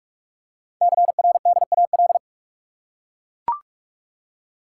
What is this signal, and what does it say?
Morse code, heard as a keyed tone, about 44 wpm, 700 Hz. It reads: CRDAL
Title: Code Groups - 5 Letters - 45wpm